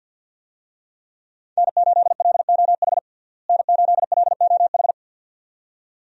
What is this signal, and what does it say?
Morse code, heard as a keyed tone, about 51 wpm, 700 Hz. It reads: N8LOH N8LOH
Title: USA Call Signs - 5 Characters - Repeated 2x - 50wpm